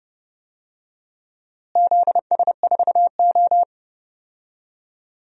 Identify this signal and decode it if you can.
Morse code, heard as a keyed tone, about 29 wpm, 700 Hz. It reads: ZS4O
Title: DXCC-100 Call Signs - 4 Characters - Repeated 3x - 30wpm